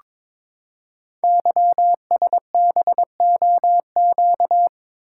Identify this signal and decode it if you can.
Morse code, heard as a keyed tone, about 22 wpm, 700 Hz. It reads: YSBOQ
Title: Code Groups - 5 Letters - 22wpm